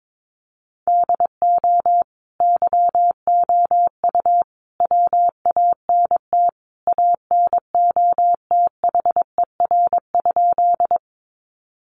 Morse code, heard as a keyed tone, about 22 wpm, 700 Hz. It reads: DO YOU WANT ANOTHER?
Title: Sentences from Top 600 Words 22wpm